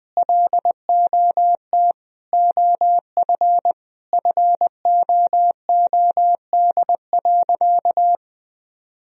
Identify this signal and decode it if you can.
Morse code, heard as a keyed tone, about 21 wpm, 700 Hz. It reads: LOT OF FOOD.
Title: Sentences from Top 1000 Words - Review Part 3 - 20wpm